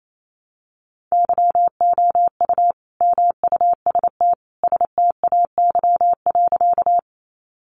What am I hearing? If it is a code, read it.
Morse code, heard as a keyed tone, about 28 wpm, 700 Hz. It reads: YOU MUST STAY.
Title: Sentences from Top 500 Words 28wpm